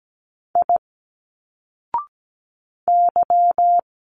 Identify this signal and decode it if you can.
Morse code, heard as a keyed tone, about 17 wpm, 700 Hz. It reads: I Y
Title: Single Letters 17wpm